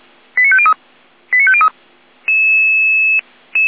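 Five-tone selective call (ZVEI), heard as quick runs of five tones, tones of about 70 ms. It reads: 89503 89503
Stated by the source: ZVEI five-tone call received on SDR.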